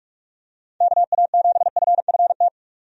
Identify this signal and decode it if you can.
Morse code, heard as a keyed tone, about 47 wpm, 700 Hz. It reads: KA7FFT
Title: USA Call Signs - 6 Characters - 45wpm